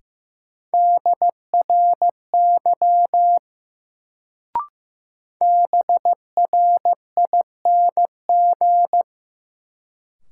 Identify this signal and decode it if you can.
Morse code, heard as a keyed tone, about 15 wpm, 700 Hz. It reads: DRY BRING
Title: Sentences from Top 500 Words - New Word Drill - 15wpm